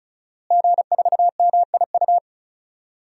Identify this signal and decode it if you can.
Morse code, heard as a keyed tone, about 34 wpm, 700 Hz. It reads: G4MIU
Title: DXCC-100 Call Signs - 5 Characters - Repeated 2x - 35wpm